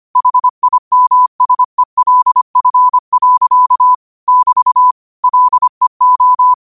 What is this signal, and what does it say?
Morse code, heard as a keyed tone, about 25 wpm, 1000 Hz. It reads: HIMSELF. <BT> LEO
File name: qotdfiles / 25wpm